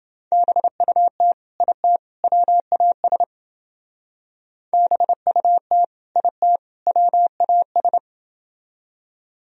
Morse code, heard as a keyed tone, about 28 wpm, 700 Hz. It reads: BUT IT WAS BUT IT WAS